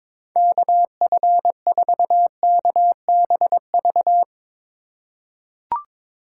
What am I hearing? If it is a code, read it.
Morse code, heard as a keyed tone, about 21 wpm, 700 Hz. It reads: KF4KBV
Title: USA Call Signs - 6 Characters - 3x Character Spacing - 22wpm